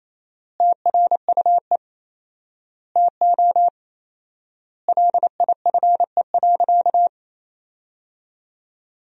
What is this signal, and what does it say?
Morse code, heard as a keyed tone, about 27 wpm, 700 Hz. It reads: TRUE TO LIFE.
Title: 4x Word-Spacing - Sentences from Top 400 Words - 28wpm